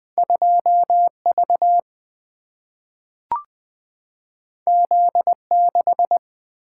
Morse code, heard as a keyed tone, about 20 wpm, 700 Hz. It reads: 2 V Z 6
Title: Single-Letter Single-Number 20wpm